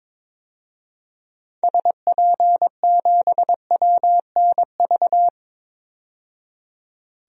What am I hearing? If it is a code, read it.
Morse code, heard as a keyed tone, about 22 wpm, 700 Hz. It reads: SP7WNV
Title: DXCC-100 Call Signs - 6 Characters - 22wpm